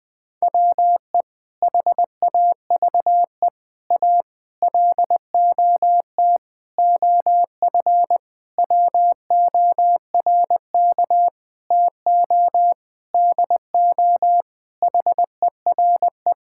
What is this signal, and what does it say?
Morse code, heard as a keyed tone, about 21 wpm, 700 Hz. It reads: WE HAVE A LOT OF WORK TO DO HERE
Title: Sentences with 7-Word Phrases - 20wpm